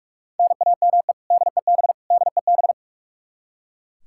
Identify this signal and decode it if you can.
Morse code, heard as a keyed tone, about 41 wpm, 700 Hz. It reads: NAME DEB DEB
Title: Practice QSO - Exchange 1 - Beginning - 45wpm